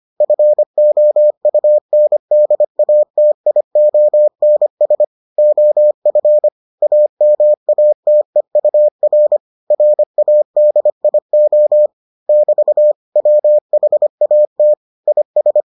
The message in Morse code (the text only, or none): FOUNDATIONS OF AMATEUR RADIO <BT> WHAT IS